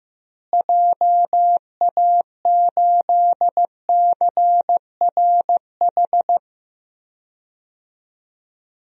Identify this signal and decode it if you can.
Morse code, heard as a keyed tone, about 15 wpm, 700 Hz. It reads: JA8CRH
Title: DXCC-100 Call Signs - 6 Characters - Repeated 3x - 15wpm